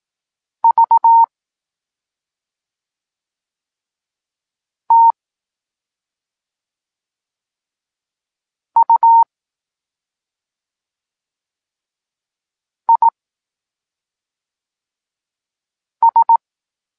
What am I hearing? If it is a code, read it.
Morse code, heard as a keyed tone, about 18 wpm, spaced as for 2 wpm, 920 Hz. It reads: VTUIS
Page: Tekst 8 – doskonalenie odbioru poznanych liter